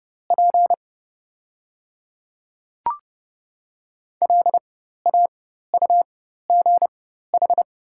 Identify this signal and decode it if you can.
Morse code, heard as a keyed tone, about 30 wpm, spaced as for 14 wpm, 690 Hz. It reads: P LAUGH
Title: 4x Character Spacing - Top 500 Words - 30wpm